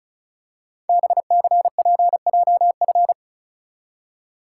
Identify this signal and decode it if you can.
Morse code, heard as a keyed tone, about 36 wpm, 690 Hz. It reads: BCPJF
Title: Code Groups - 5 Letters - 35wpm